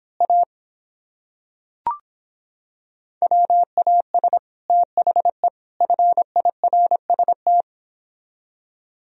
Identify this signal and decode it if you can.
Morse code, heard as a keyed tone, about 26 wpm, 700 Hz. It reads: A WAS THE FIRST